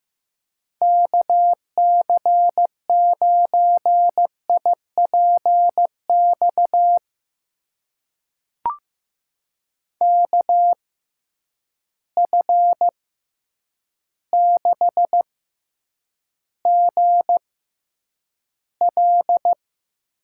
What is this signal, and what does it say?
Morse code, heard as a keyed tone, about 15 wpm, spaced as for 8 wpm, 700 Hz. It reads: KC9IPX KF6GL
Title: USA Call Signs - 6 Characters - 6x Character Spacing - 15wpm